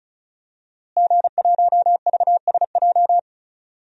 Morse code, heard as a keyed tone, about 36 wpm, 700 Hz. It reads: G1VSJ
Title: DXCC-100 Call Signs - 5 Characters - Repeated 2x - 35wpm